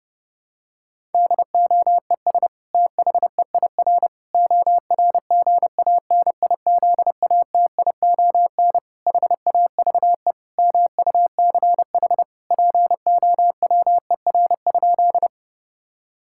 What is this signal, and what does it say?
Morse code, heard as a keyed tone, about 32 wpm, 700 Hz. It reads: DOES THEIR ORGANIZATION HAVE MUCH POWER?
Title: Sentences from Top 1000 Words - Review Part 5 - 30wpm